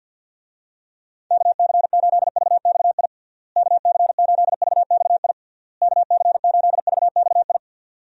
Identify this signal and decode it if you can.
Morse code, heard as a keyed tone, about 50 wpm, 700 Hz. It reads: KC8VXI KC8VXI KC8VXI
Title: USA Call Signs - 6 Characters - Repeated 3x - 50wpm